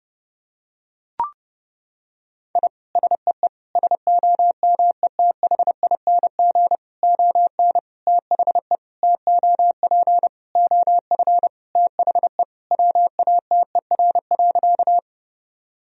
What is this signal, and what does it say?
Morse code, heard as a keyed tone, about 30 wpm, 700 Hz. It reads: I SEE SOMETHING ON THE TOP OF THE WATER.
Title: Sentences from Top 1000 Words - Review Part 2 - 30wpm